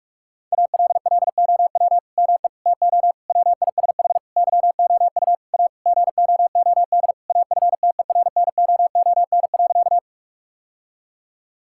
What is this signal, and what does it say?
Morse code, heard as a keyed tone, about 47 wpm, 700 Hz. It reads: ALLOW ME TO WISH YOU A GOOD AFTERNOON.
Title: Sentences from Top 1600 Words 45wpm